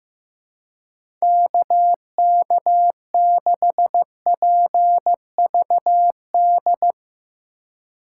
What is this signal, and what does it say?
Morse code, heard as a keyed tone, about 15 wpm, 700 Hz. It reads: KK6PVD
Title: USA Call Signs - 6 Characters - 3x Character Spacing - 15wpm